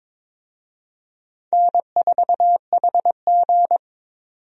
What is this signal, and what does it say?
Morse code, heard as a keyed tone, about 22 wpm, 700 Hz. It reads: N4HG
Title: USA Call Signs - 4 Characters - 22wpm